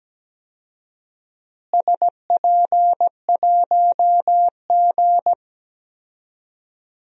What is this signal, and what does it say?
Morse code, heard as a keyed tone, about 17 wpm, 700 Hz. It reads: SP1G
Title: DXCC-100 Call Signs - 4 Characters - Repeated 2x - 17wpm